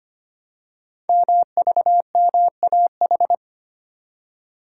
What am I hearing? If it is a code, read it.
Morse code, heard as a keyed tone, about 25 wpm, 700 Hz. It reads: MVMAH
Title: Code Groups - 5 Letters - 25wpm